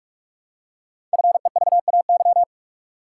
Morse code, heard as a keyed tone, about 44 wpm, 700 Hz. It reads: FEVAY
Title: Code Groups - 5 Letter:Numbers - 45wpm